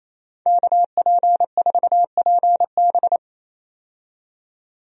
Morse code, heard as a keyed tone, about 28 wpm, 700 Hz. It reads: KP4PB
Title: DXCC-100 Call Signs - 5 Characters - 28wpm